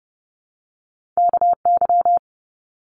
Morse code, heard as a keyed tone, about 30 wpm, 700 Hz. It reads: KY
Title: US States 30wpm